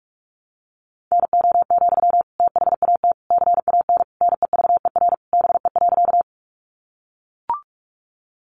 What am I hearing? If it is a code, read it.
Morse code, heard as a keyed tone, about 44 wpm, 700 Hz. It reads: NO, THAT CAN NEVER BE.
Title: Sentences from Top 300 Words 45wpm